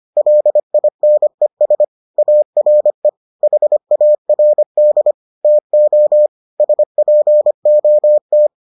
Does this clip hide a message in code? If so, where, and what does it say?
Yes: Morse code from 0.2 to 8.5 s, heard as a keyed tone, about 25 wpm, 600 Hz. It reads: LINES ARE HARD TO SPOT